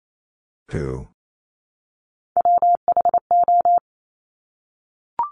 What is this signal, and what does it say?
Morse code, heard as a keyed tone, about 28 wpm, 700 Hz. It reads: WHO
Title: Top 200 Words Repeated 28wpm